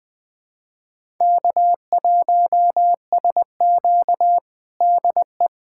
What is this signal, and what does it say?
Morse code, heard as a keyed tone, about 21 wpm, 700 Hz. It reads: K1SQ DE
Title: USA Callsign DE Callsign - 4 Characters - 20wpm